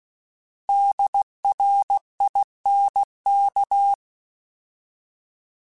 Morse code, heard as a keyed tone, about 16 wpm, 800 Hz. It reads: DRINK